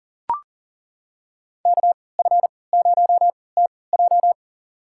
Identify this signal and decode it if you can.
Morse code, heard as a keyed tone, about 40 wpm, spaced as for 23 wpm, 700 Hz. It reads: KF0TJ
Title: USA Call Signs - 5 Characters - 3x Character Spacing - 40wpm